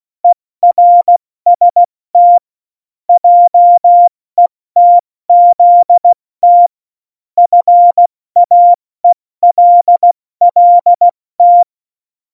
Morse code, heard as a keyed tone, about 16 wpm, 700 Hz. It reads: ERST JETZT FAELLT
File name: Morsekurs / Witze / Witz 22. Dezember 2020, 70 F80